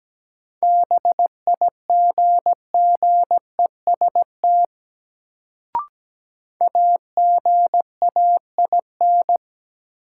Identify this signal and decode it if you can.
Morse code, heard as a keyed tone, about 17 wpm, 700 Hz. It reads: BIGGEST AGAIN
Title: Sets of 4 Words - New Word Drill - 17wpm